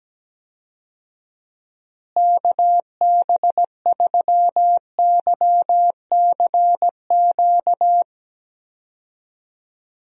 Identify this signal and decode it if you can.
Morse code, heard as a keyed tone, about 17 wpm, 700 Hz. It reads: KB3YCQ
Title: USA Call Signs - 6 Characters - Repeated 3x - 17wpm